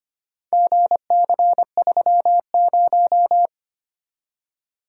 Morse code, heard as a keyed tone, about 25 wpm, 700 Hz. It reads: GC30
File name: practice / 25wpm / Four Letter-Number Combinations - 25wpm